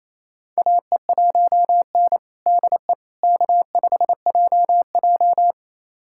Morse code, heard as a keyed tone, about 28 wpm, 700 Hz. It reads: AE1N DE K5JJ
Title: USA Callsign DE Callsign - 4 Characters - 28wpm